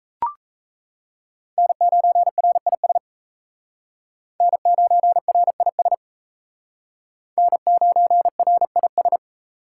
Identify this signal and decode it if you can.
Morse code, heard as a keyed tone, about 35 wpm, 700 Hz. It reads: N9RIS N9RIS N9RIS